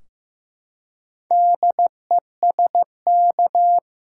Morse code, heard as a keyed tone, about 16 wpm, 700 Hz. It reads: DESK
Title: Four Letter Words - 15wpm